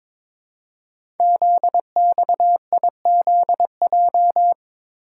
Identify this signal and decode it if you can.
Morse code, heard as a keyed tone, about 22 wpm, 700 Hz. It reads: ZXIZJ
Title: Code Groups - 5 Letters - 22wpm